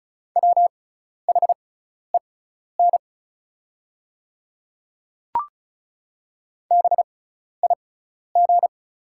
Morse code, heard as a keyed tone, about 35 wpm, spaced as for 12 wpm, 690 Hz. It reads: WHEN BIG
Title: 6x Character Spacing - Top 200 Words - 35wpm